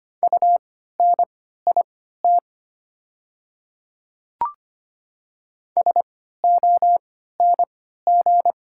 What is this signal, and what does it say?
Morse code, heard as a keyed tone, about 25 wpm, spaced as for 14 wpm, 690 Hz. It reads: UNIT SONG
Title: 3x Character Spacing - Top 500 Words - 25wpm